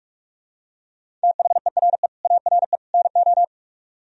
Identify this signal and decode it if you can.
Morse code, heard as a keyed tone, about 44 wpm, 700 Hz. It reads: THERE ARE NO